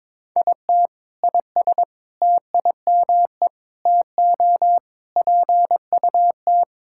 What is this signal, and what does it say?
Morse code, heard as a keyed tone, about 21 wpm, 700 Hz. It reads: IT IS TIME TO PUT
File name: Sentences with 7-Word Phrases - 22wpm